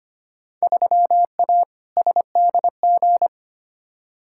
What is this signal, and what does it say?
Morse code, heard as a keyed tone, about 25 wpm, 700 Hz. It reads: 3A SDG
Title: Field Day Exchanges - 25wpm